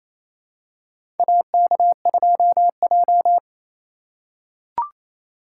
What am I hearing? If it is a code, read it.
Morse code, heard as a keyed tone, about 28 wpm, 700 Hz. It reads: AK2J
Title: USA Call Signs - 4 Characters - 3x Character Spacing - 28wpm